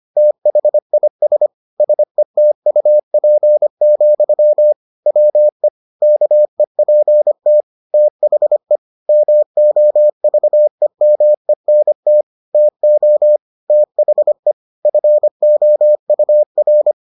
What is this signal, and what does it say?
Morse code, heard as a keyed tone, about 25 wpm, 600 Hz. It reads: THIS SETUP, WE KEPT THE MOVEMENT TO THE FOUR